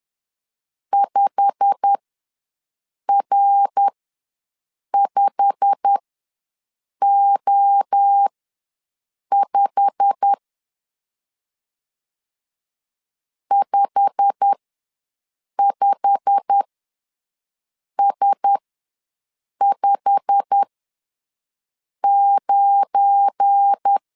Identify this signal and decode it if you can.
Morse code, heard as a keyed tone, about 10 wpm, spaced as for 6 wpm, 800 Hz. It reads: 5R5O5 55S59
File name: LEZIONI / 8 YJ45 / 8 YJ45 ws 10wpm 3843